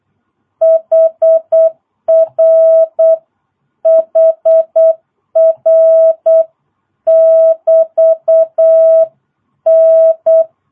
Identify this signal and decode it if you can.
Morse code, heard as a keyed tone, about 8 wpm, 650 Hz. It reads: HR HR <BT> N
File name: jh8cbh / 2022cw